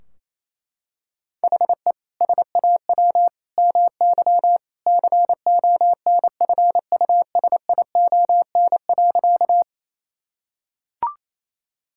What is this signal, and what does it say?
Morse code, heard as a keyed tone, about 28 wpm, 700 Hz. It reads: HE SAW MY CONFUSION.
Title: Sentences from Top 1600 Words 28wpm